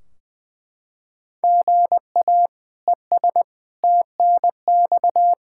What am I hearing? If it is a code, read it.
Morse code, heard as a keyed tone, about 20 wpm, 700 Hz. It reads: GA ES TNX
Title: Practice QSO - Exchange 1 - Beginning - 20wpm